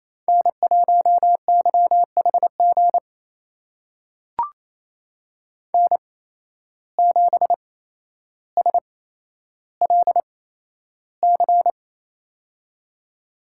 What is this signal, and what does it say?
Morse code, heard as a keyed tone, about 28 wpm, spaced as for 12 wpm, 700 Hz. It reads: N1YHG N7SLC
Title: USA Call Signs - 5 Characters - 8x Character Spacing - 28wpm